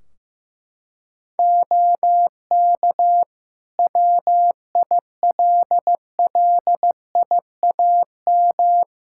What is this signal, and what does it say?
Morse code, heard as a keyed tone, about 16 wpm, 700 Hz. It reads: OK WILLIAM
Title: Practice QSO - Exchange 3 - More Info - 15wpm